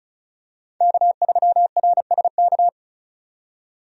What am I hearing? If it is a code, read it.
Morse code, heard as a keyed tone, about 34 wpm, 700 Hz. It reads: K3RSK
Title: USA Call Signs - 5 Characters - 35wpm